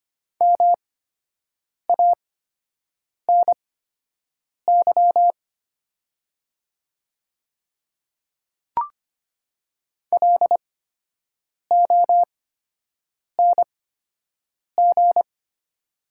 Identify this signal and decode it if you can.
Morse code, heard as a keyed tone, about 25 wpm, spaced as for 7 wpm, 700 Hz. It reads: MANY LONG
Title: 8x Character Spacing - Top 100 Words - 25wpm